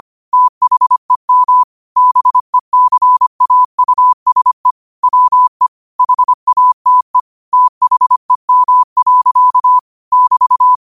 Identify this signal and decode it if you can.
Morse code, heard as a keyed tone, about 25 wpm, 1000 Hz. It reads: THEM BECAUSE WE HATE THEM. <BT>
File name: qotdfiles / 25wpm